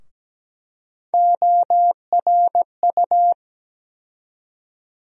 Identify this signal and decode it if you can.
Morse code, heard as a keyed tone, about 17 wpm, 700 Hz. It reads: ORU
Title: USA Call Sign Postfixes - 3 Letters 17wpm